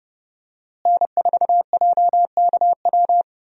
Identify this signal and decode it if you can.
Morse code, heard as a keyed tone, about 30 wpm, 700 Hz. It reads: N4JKW
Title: USA Call Signs - 5 Characters - Repeated 3x - 30wpm